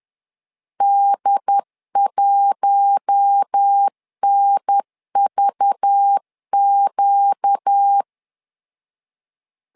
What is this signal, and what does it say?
Morse code, heard as a keyed tone, about 10 wpm, 800 Hz. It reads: D1NVQ
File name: LEZIONI / 5 LBVC / 5 LBV 10wpm 2248